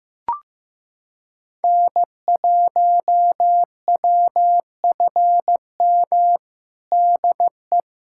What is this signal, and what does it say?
Morse code, heard as a keyed tone, about 15 wpm, 700 Hz. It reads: N1WFM DE
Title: USA Callsign DE Callsign - 5 Characters - 15wpm